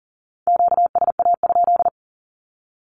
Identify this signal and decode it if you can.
Morse code, heard as a keyed tone, about 40 wpm, 700 Hz. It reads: QSA?
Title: Q-Codes Repeated 3x 40wpm